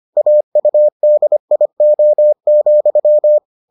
Morse code, heard as a keyed tone, about 25 wpm, 600 Hz. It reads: AUDIO,